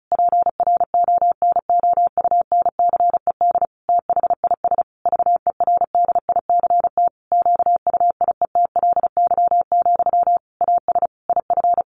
Morse code, heard as a keyed tone, about 34 wpm, 700 Hz. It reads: PRONOUNCED THIS VERDICT QUIETLY, AS IF
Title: A Sound Of Thunder 35wpm